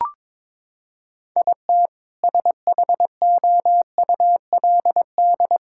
Morse code, heard as a keyed tone, about 22 wpm, 700 Hz. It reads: IT SHOULD
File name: practice / 22wpm / Sets of 8 Words - 22wpm